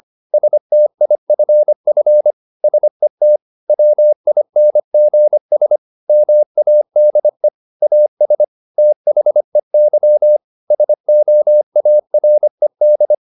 Morse code, heard as a keyed tone, about 25 wpm, 600 Hz. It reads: STIFF SET WINGS MADE AS THEY SOARED
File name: the old man and the sea part 1 25wpm